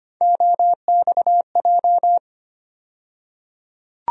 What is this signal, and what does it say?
Morse code, heard as a keyed tone, about 25 wpm, 700 Hz. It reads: OXJ
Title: USA Call Sign Postfixes - 3 Letters 25wpm